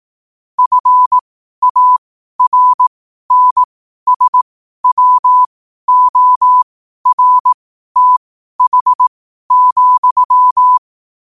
Morse code, heard as a keyed tone, about 18 wpm, spaced as for 12 wpm, 1000 Hz. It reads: FARNSWORTH,
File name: qotdfiles / 12wpm